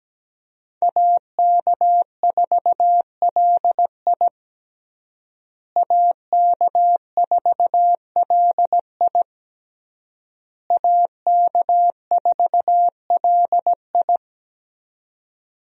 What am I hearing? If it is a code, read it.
Morse code, heard as a keyed tone, about 17 wpm, 700 Hz. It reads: AK4LI AK4LI AK4LI
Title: USA Call Signs - 5 Characters - Repeated 3x - 17wpm